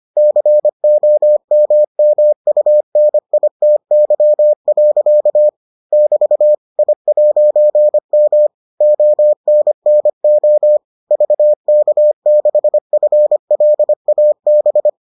Morse code, heard as a keyed tone, about 25 wpm, 600 Hz. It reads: COMMUNITY. <BT> I'M ONNO VK6FLAB